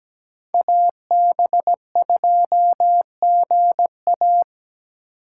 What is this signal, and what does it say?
Morse code, heard as a keyed tone, about 17 wpm, 700 Hz. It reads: AB2GA
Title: USA Call Signs - 5 Characters - 8x Character Spacing - 17wpm